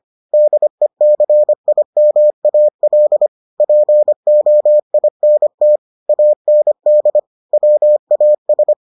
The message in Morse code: DECIMAL POINT AND WAS

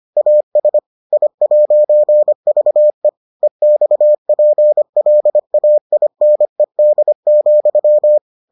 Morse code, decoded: AS I'VE EXPLAINED,

